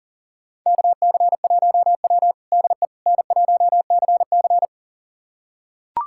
Morse code, heard as a keyed tone, about 40 wpm, 700 Hz. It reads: KC1W DE N1CC
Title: USA Callsign DE Callsign - 4 Characters - 40wpm